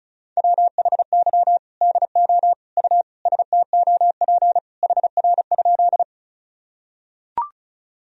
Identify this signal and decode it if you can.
Morse code, heard as a keyed tone, about 36 wpm, 700 Hz. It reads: WHY DO U STOP HR?
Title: Sentences from Top 300 Words and Abbreviations 35wpm